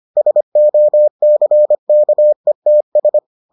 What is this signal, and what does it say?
Morse code, heard as a keyed tone, about 25 wpm, 600 Hz. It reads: SOCKETS